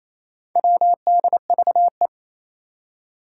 Morse code, heard as a keyed tone, about 28 wpm, 700 Hz. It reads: WDVE
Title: Four Challenging Letter-Number Combinations 28wpm